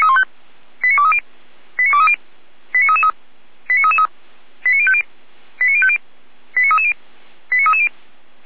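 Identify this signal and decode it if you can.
Five-tone selective call (ZVEI), heard as quick runs of five tones, tones of about 70 ms. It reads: 89230 89230 89333 89333 89050 89050 89330 89330
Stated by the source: ZVEI five-tone selective call 89320, received by SDR.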